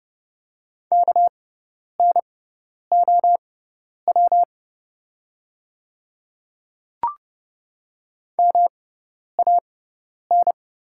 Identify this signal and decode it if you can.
Morse code, heard as a keyed tone, about 30 wpm, spaced as for 10 wpm, 700 Hz. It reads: KNOW MAN
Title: 6x Character Spacing - Top 300 Words - 30wpm